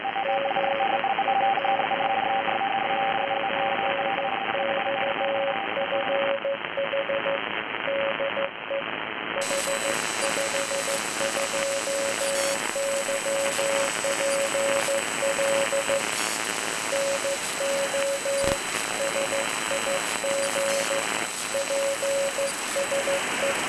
Morse code, heard as a keyed tone, about 14 wpm, 550 Hz, with noise. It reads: G29XFH DE H52YPL NO SIG PSE